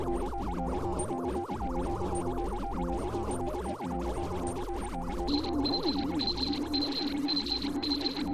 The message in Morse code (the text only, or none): none